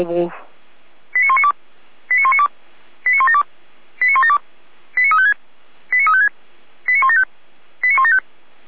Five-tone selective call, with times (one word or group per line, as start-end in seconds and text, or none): 1.2-1.5 89192
2.1-2.5 89192
3.1-3.4 89172
4.0-4.4 89172
5.0-5.3 89356
5.9-6.3 89356
6.9-7.2 89175
7.8-8.2 89175